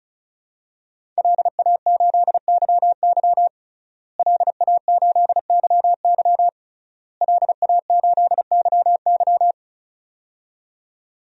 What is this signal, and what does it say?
Morse code, heard as a keyed tone, about 35 wpm, 700 Hz. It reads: LA8YY LA8YY LA8YY
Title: DXCC-100 Call Signs - 5 Characters - Repeated 3x - 35wpm